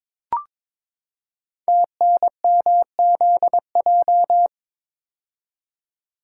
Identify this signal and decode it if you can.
Morse code, heard as a keyed tone, about 23 wpm, 700 Hz. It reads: TNMZJ
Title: Code Groups - 5 Letters - 22wpm